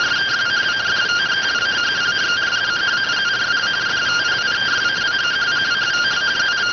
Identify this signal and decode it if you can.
Two-tone teleprinter signal, about 170 Hz shift.